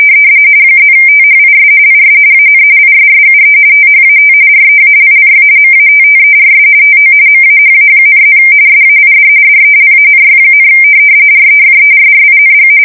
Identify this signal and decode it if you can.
RTTY, about 45.45 baud. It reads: +THIS IS AN ISSUE THAT REALLY MATTERS TO MAHA,+ SAID DELANY, REFERRING
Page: 170 Shift 2125/2295